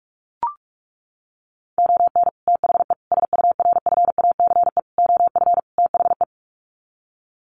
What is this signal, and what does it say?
Morse code, heard as a keyed tone, about 44 wpm, 700 Hz. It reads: ON THE SURFACE OF THE
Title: Sets of 5 Words Encore 45wpm